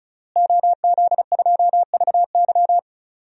Morse code, heard as a keyed tone, about 36 wpm, 700 Hz. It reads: OZ2VY
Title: DXCC-100 Call Signs - 5 Characters - Repeated 3x - 35wpm